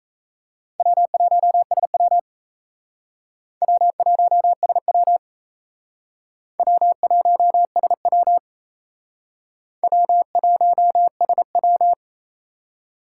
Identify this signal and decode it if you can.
Morse code, heard as a keyed tone, about 36 wpm, 700 Hz. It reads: W1SW W1SW W1SW W1SW